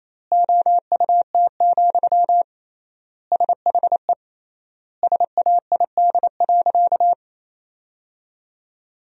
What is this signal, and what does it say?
Morse code, heard as a keyed tone, about 28 wpm, 700 Hz. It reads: OUT, SHE SAID.
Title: 3x Word-Spacing - Sentences from Top 500 Words - 28wpm